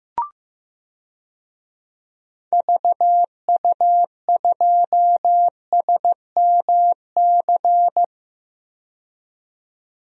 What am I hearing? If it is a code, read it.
Morse code, heard as a keyed tone, about 15 wpm, 700 Hz. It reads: VU2SMC
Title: DXCC-100 Call Signs - 6 Characters - 15wpm